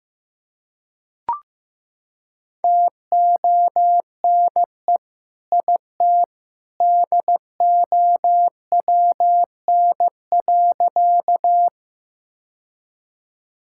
Morse code, heard as a keyed tone, about 15 wpm, 700 Hz. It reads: TONE IT DOWN.
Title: Sentences from Top 1600 Words 15wpm